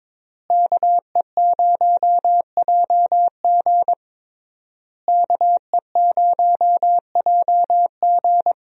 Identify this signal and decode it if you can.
Morse code, heard as a keyed tone, about 22 wpm, 700 Hz. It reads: KE0JG KE0JG
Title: USA Call Signs - 5 Characters - Repeated 3x - 22wpm